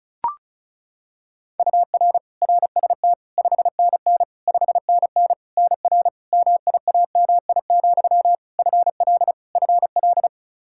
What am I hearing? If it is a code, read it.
Morse code, heard as a keyed tone, about 36 wpm, 700 Hz. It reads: UR RST 5NN 5NN NR MIAMI, FL FL
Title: K1USN SST QSO Practice 35wpm